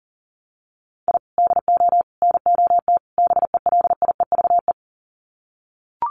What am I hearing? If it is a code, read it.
Morse code, heard as a keyed tone, about 40 wpm, 700 Hz. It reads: I DO NOT BELIEVE
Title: Sets of 4 Words 40wpm